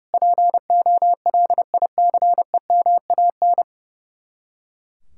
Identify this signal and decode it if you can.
Morse code, heard as a keyed tone, about 38 wpm, 700 Hz. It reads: POLICEMAN